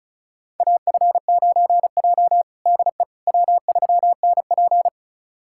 Morse code, heard as a keyed tone, about 36 wpm, 700 Hz. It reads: AF9J DE W3NP
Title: USA Callsign DE Callsign - 4 Characters - 35wpm